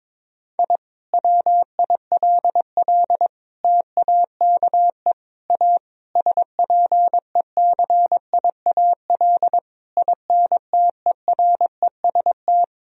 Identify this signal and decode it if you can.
Morse code, heard as a keyed tone, about 21 wpm, 700 Hz. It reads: I WILL TAKE A SPECIAL INTEREST
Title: Sentences from Top 1000 Words - Review Part 3 - 22wpm